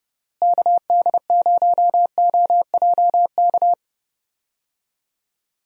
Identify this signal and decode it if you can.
Morse code, heard as a keyed tone, about 32 wpm, 700 Hz. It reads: KD0OJK
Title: USA Call Signs - 6 Characters - Repeated 3x - 30wpm